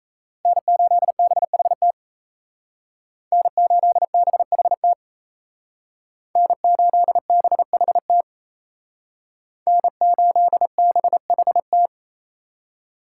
Morse code, heard as a keyed tone, about 36 wpm, 700 Hz. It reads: N8BHT N8BHT N8BHT N8BHT